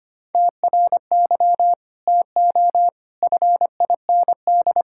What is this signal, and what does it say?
Morse code, heard as a keyed tone, about 25 wpm, 700 Hz. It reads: TRY TO FIND
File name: practice / 25wpm / Sentences with 2-Word Phrases - 25wpm